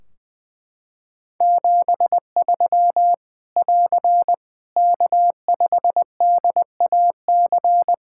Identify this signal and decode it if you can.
Morse code, heard as a keyed tone, about 21 wpm, 700 Hz. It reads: 73 <AR> K5DAC
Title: Practice QSO - Exchange 4 - Ending - 20wpm